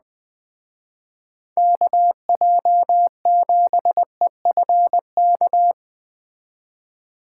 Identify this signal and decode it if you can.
Morse code, heard as a keyed tone, about 21 wpm, 700 Hz. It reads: KJ7EFK
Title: Speed Racing - USA 2x3 Call Signs 17wpm